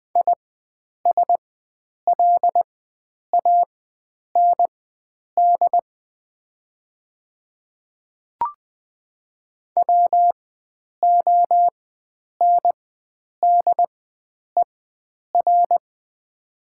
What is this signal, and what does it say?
Morse code, heard as a keyed tone, about 20 wpm, spaced as for 9 wpm, 700 Hz. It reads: ISLAND WONDER
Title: 4x Character Spacing - Top 500 Words - 20wpm